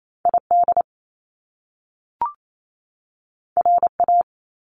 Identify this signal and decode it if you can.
Morse code, heard as a keyed tone, about 28 wpm, 700 Hz. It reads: ID RA
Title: Top Two Letter Combinations in Words 28wpm